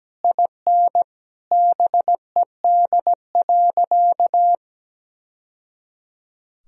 Morse code, heard as a keyed tone, about 17 wpm, 700 Hz. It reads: IN BED.
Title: Sentences from Top 1000 Words - Review Part 1 - 17wpm